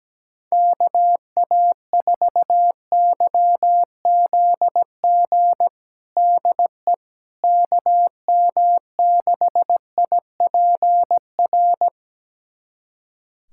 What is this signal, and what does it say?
Morse code, heard as a keyed tone, about 17 wpm, 700 Hz. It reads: KA4YZG DE KM6IPR
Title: USA Callsign DE Callsign - 6 Characters - 17wpm